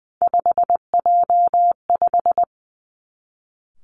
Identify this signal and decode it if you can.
Morse code, heard as a keyed tone, about 20 wpm, 700 Hz. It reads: 5J5
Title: Three Challenging Letter-Number Combinations 20wpm